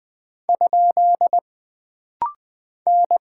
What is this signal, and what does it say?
Morse code, heard as a keyed tone, about 20 wpm, 700 Hz. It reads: ? N
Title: Single Character (Letter:Number:Punctuation:Prosign) - Rapid-Fire - 20wpm